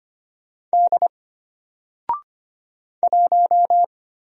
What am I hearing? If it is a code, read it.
Morse code, heard as a keyed tone, about 25 wpm, 700 Hz. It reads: D 1
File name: Single-Letter-Number-25wpm